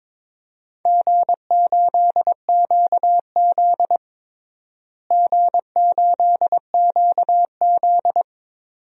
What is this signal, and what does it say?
Morse code, heard as a keyed tone, about 22 wpm, 700 Hz. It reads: G8QZ G8QZ
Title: DXCC-100 Call Signs - 4 Characters - Repeated 2x - 22wpm